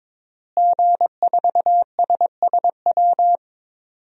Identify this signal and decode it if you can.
Morse code, heard as a keyed tone, about 23 wpm, 700 Hz. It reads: G4SSW
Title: DXCC-100 Call Signs - 5 Characters - Repeated 3x - 22wpm